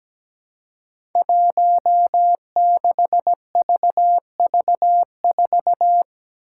Morse code, heard as a keyed tone, about 17 wpm, 700 Hz. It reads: 16VV4
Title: Five Challenging Letter-Number Combinations 17wpm